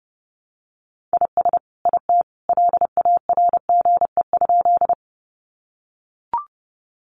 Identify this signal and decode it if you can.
Morse code, heard as a keyed tone, about 30 wpm, 700 Hz. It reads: IS IT LARGE?
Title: Sentences from Top 200 Words 30wpm